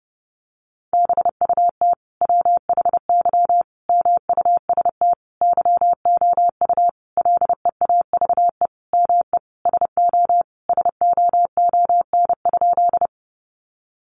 Morse code, heard as a keyed tone, about 30 wpm, 700 Hz. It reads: BUT WHY MUST YOU LEAVE ME SO SOON?
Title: Sentences from Top 400 Words 30wpm